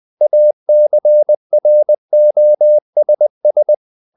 Morse code, heard as a keyed tone, about 20 wpm, 600 Hz. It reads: ACROSS
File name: the old man and the sea / the old man and the sea part 1 20wpm